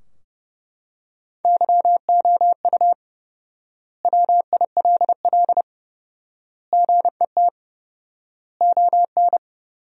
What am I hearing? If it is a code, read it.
Morse code, heard as a keyed tone, about 32 wpm, 700 Hz. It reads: YOU WILL GET ON
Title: 4x Word-Spacing - Sentences from Top 200 Words - 30wpm